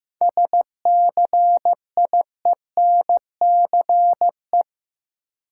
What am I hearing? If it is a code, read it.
Morse code, heard as a keyed tone, about 15 wpm, 700 Hz. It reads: SCIENCE
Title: Sentences from Top 1000 Words - Review Part 2 - New Word Drill - 15wpm